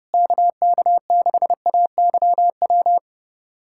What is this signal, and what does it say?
Morse code, heard as a keyed tone, about 30 wpm, 700 Hz. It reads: KK6AYW
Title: USA Callsign DE Callsign - 6 Characters - 30wpm